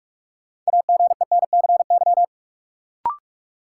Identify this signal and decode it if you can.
Morse code, heard as a keyed tone, about 46 wpm, 700 Hz. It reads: AGENCY
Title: Sentences from Top 1300 Words - New Word Drill Alt - 45wpm